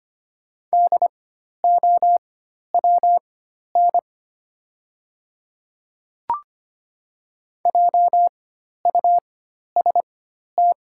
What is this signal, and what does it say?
Morse code, heard as a keyed tone, about 25 wpm, spaced as for 12 wpm, 690 Hz. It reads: DOWN JUST
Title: 4x Character Spacing - Top 200 Words - 25wpm